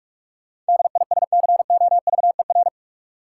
Morse code, heard as a keyed tone, about 43 wpm, 700 Hz. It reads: DISCOVER
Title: Sentences from Top 1100 Words - New Word Drill - 45wpm